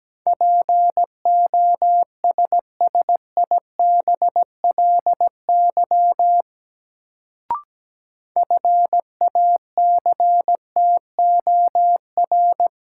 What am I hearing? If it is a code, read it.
Morse code, heard as a keyed tone, about 17 wpm, 700 Hz. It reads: POSSIBLY FACTOR
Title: Sentences from Top 1300 Words - New Word Drill - 17wpm